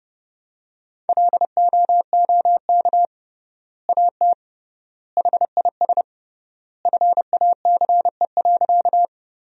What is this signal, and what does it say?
Morse code, heard as a keyed tone, about 30 wpm, 700 Hz. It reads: LOOK AT HIS FACE.
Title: 3x Word-Spacing - Sentences from Top 300 Words - 30wpm